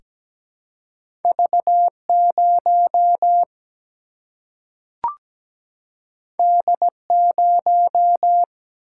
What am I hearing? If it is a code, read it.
Morse code, heard as a keyed tone, about 17 wpm, 700 Hz. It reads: V0 D0
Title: Single-Letter Single-Number 17wpm